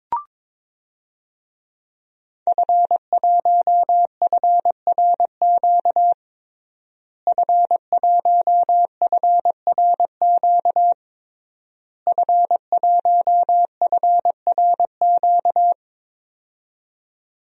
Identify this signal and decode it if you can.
Morse code, heard as a keyed tone, about 22 wpm, 700 Hz. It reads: F1FRQ F1FRQ F1FRQ
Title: DXCC-100 Call Signs - 5 Characters - Repeated 3x - 22wpm